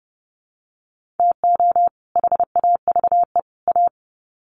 Morse code, heard as a keyed tone, about 30 wpm, 700 Hz. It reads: TO HAVE A
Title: Sets of 3 Words 30wpm